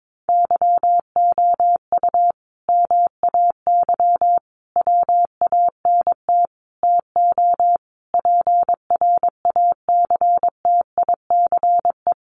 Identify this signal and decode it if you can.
Morse code, heard as a keyed tone, about 22 wpm, 700 Hz. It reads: YOU MAY WANT TO PRACTICE
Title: Sentences from Top 700 Words 22wpm